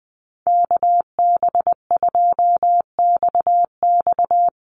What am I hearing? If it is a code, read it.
Morse code, heard as a keyed tone, about 20 wpm, 700 Hz. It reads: KB2XX
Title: NA SOTA Chaser Callsigns 20wpm